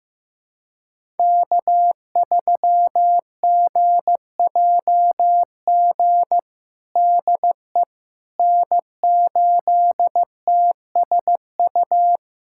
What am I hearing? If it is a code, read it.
Morse code, heard as a keyed tone, about 15 wpm, 700 Hz. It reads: K3GJG DE N8TSU
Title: USA Callsign DE Callsign - 5 Characters - 15wpm